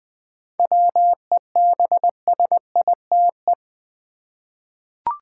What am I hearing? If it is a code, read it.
Morse code, heard as a keyed tone, about 20 wpm, 690 Hz. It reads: WEBSITE
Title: 7 Letter Words - First 3 Letters Given - 20wpm